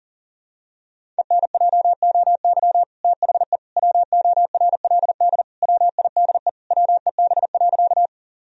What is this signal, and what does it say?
Morse code, heard as a keyed tone, about 40 wpm, 700 Hz. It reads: ENJOY THE WORLD WIDE WEB.
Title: Sentences from Top 1000 Words - Review Part 4 - 40wpm